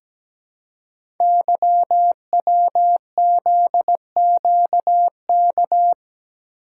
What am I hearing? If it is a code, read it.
Morse code, heard as a keyed tone, about 17 wpm, 700 Hz. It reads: YWZQK
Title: Code Groups - 5 Letters - 17wpm